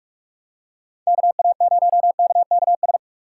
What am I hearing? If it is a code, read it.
Morse code, heard as a keyed tone, about 46 wpm, 700 Hz. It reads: KA0KKS
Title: USA Call Signs - 6 Characters - 4x Character Spacing - 45wpm